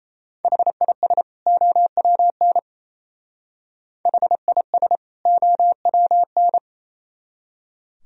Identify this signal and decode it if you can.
Morse code, heard as a keyed tone, about 35 wpm, 700 Hz. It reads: HIS OWN HIS OWN